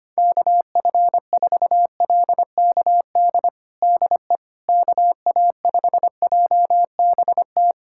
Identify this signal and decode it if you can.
Morse code, heard as a keyed tone, about 25 wpm, 700 Hz. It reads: KF4LKD DE KA5JBT
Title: USA Callsign DE Callsign - 6 Characters - 25wpm